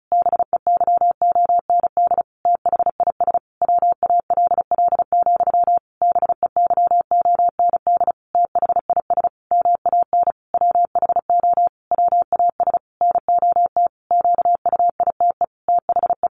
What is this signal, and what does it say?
Morse code, heard as a keyed tone, about 34 wpm, 700 Hz. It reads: BEYOND THIS WALL, BEYOND THIS MAN WHO WAS NOT QUITE THE
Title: A Sound Of Thunder 35wpm